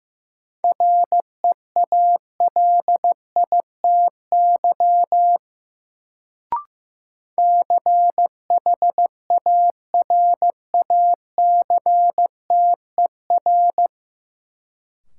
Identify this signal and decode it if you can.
Morse code, heard as a keyed tone, about 15 wpm, 700 Hz. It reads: REALITY CHARACTER
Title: Sentences from Top 1200 Words - New Word Drill - 15wpm